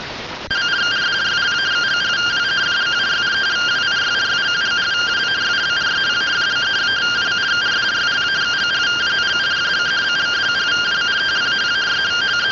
Two-tone teleprinter signal, about 170 Hz shift.